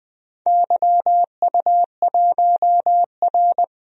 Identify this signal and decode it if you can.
Morse code, heard as a keyed tone, about 20 wpm, 700 Hz. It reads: YU1R
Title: DXCC-100 Call Signs - 4 Characters - 20wpm